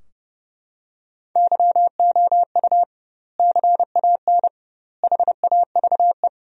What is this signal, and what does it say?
Morse code, heard as a keyed tone, about 30 wpm, 700 Hz. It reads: YOU CAN HAVE
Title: 2x Word-Spacing - Sentences from Top 100 Words - 30wpm